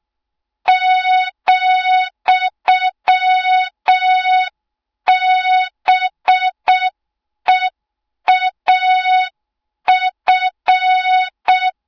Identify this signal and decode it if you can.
Morse code, heard as a keyed tone, about 6 wpm, 3700 Hz. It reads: ,BEAF
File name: Morze-HA2MN / hang / 20 GYAKORLAT